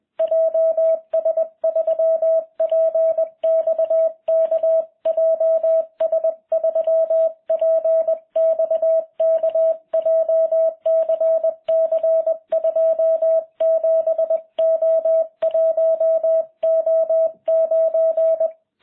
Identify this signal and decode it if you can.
Morse code, heard as a keyed tone, about 20 wpm, 660 Hz. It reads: JS3PXK JS3PXK JCC 27O1O9